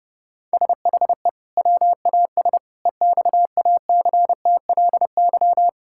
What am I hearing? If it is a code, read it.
Morse code, heard as a keyed tone, about 30 wpm, 700 Hz. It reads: SHE WAS EXACTLY
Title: Sentences from Top 1000 Words - Review Part 6 - 30wpm